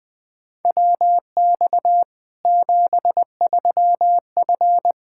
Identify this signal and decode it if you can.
Morse code, heard as a keyed tone, about 20 wpm, 700 Hz. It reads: WX 73F
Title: Practice QSO - Exchange 2 - Info - 20wpm